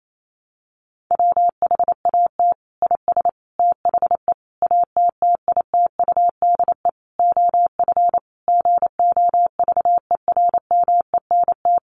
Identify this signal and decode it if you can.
Morse code, heard as a keyed tone, about 28 wpm, 700 Hz. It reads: WHAT IS THE ATTITUDE OF GOVERMENT
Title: Sentences from Top 1300 Words 28wpm